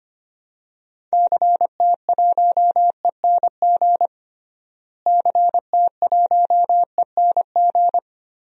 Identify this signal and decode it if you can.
Morse code, heard as a keyed tone, about 25 wpm, 700 Hz. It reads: CT1ENG CT1ENG
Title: DXCC-100 Call Signs - 6 Characters - Repeated 2x - 25wpm